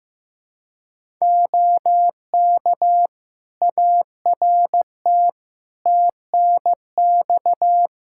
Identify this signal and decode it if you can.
Morse code, heard as a keyed tone, about 15 wpm, 700 Hz. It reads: OK ART TNX
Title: Practice QSO - Exchange 4 - Ending - 15wpm